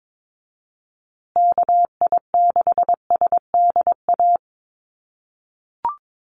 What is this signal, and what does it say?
Morse code, heard as a keyed tone, about 22 wpm, 700 Hz. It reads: KI6SDA
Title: USA Call Signs Repeated 3x 22wpm